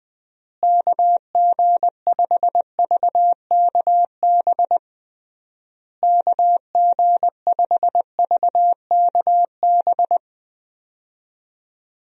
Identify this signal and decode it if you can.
Morse code, heard as a keyed tone, about 21 wpm, 700 Hz. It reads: KG5VKB KG5VKB
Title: USA Call Signs - 6 Characters - Repeated 3x - 20wpm